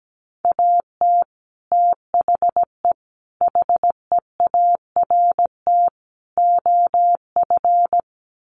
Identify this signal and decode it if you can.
Morse code, heard as a keyed tone, about 17 wpm, 700 Hz. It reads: AT THE HEART OF
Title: Sets of 4 Words 17wpm